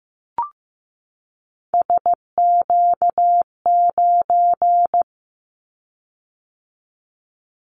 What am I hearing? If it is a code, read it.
Morse code, heard as a keyed tone, about 15 wpm, 700 Hz. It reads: SQ9
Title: Callsign Prefixes for Top 25 DXCC - 15wpm